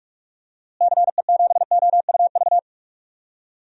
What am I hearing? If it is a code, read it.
Morse code, heard as a keyed tone, about 46 wpm, 700 Hz. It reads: CE7OUV
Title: DXCC-100 Call Signs - 6 Characters - Repeated 3x - 45wpm